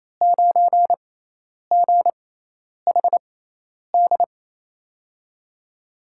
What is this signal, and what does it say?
Morse code, heard as a keyed tone, about 28 wpm, spaced as for 10 wpm, 700 Hz. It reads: 9GHD
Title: USA Call Signs - 5 Characters - 6x Character Spacing - 28wpm